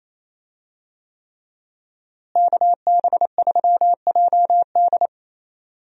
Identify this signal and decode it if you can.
Morse code, heard as a keyed tone, about 28 wpm, 700 Hz. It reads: KB3JD
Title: USA Call Signs - 5 Characters - Repeated 3x - 28wpm